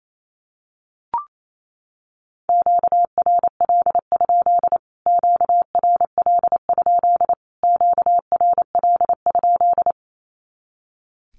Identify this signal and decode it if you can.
Morse code, heard as a keyed tone, about 28 wpm, 700 Hz. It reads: QRL? QRL? QRL?
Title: Q-Codes Repeated 3x 28wpm